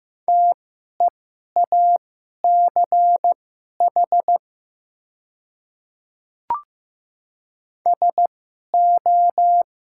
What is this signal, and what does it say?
Morse code, heard as a keyed tone, about 15 wpm, spaced as for 11 wpm, 690 Hz. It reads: TEACH SO
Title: 2x Character Spacing - Top 500 Words - 15wpm